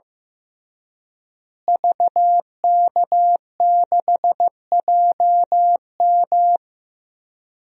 Morse code, heard as a keyed tone, about 16 wpm, 700 Hz. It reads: VK6JM
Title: DXCC-100 Call Signs - 5 Characters - Repeated 2x - 15wpm